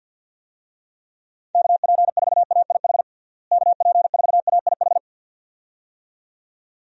Morse code, heard as a keyed tone, about 52 wpm, 700 Hz. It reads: KP4AIH KP4AIH
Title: DXCC-100 Call Signs - 6 Characters - Repeated 2x - 50wpm